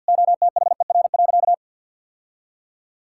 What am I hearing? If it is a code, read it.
Morse code, heard as a keyed tone, about 50 wpm, 700 Hz. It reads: OTHER.
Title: Sentences from Top 1000 Words - Review Part 6 - 50wpm